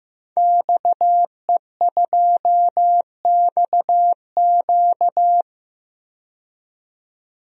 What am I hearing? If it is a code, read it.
Morse code, heard as a keyed tone, about 15 wpm, 700 Hz. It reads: XE2XQ
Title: DXCC-100 Call Signs - 5 Characters - 15wpm